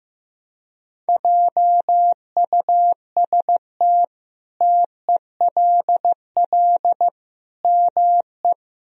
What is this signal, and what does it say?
Morse code, heard as a keyed tone, about 16 wpm, 700 Hz. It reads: JUST TELL ME
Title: Sentences from Top 1000 Words - Review Part 1 - 15wpm